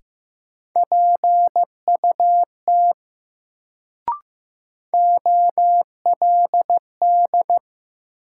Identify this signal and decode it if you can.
Morse code, heard as a keyed tone, about 15 wpm, 700 Hz. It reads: PUT OLD
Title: Sentences from Top 200 Words - New Word Drill - 15wpm